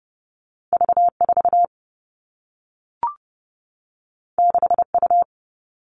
Morse code, heard as a keyed tone, about 30 wpm, 700 Hz. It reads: V4 6U
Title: All Cut Numbers - 2-Digit 30wpm